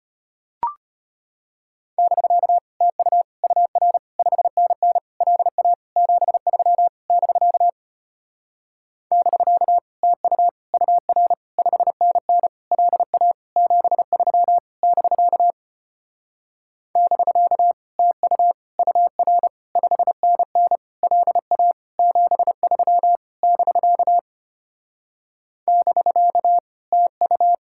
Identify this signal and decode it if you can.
Morse code, heard as a keyed tone, about 33 wpm, 700 Hz. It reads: <BK> TU UR 5NN LA 73 <BK> <BK> TU UR 5NN LA 73 <BK> <BK> TU UR 5NN LA 73 <BK> <BK> TU